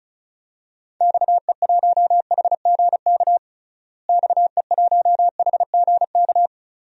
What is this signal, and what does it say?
Morse code, heard as a keyed tone, about 34 wpm, 700 Hz. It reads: XE1HGK XE1HGK
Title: DXCC-100 Call Signs - 6 Characters - Repeated 3x - 35wpm